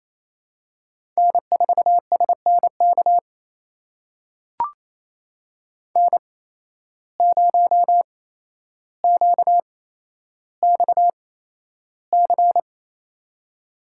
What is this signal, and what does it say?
Morse code, heard as a keyed tone, about 28 wpm, spaced as for 12 wpm, 700 Hz. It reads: N4SNK N0QXC
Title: USA Call Signs - 5 Characters - 8x Character Spacing - 28wpm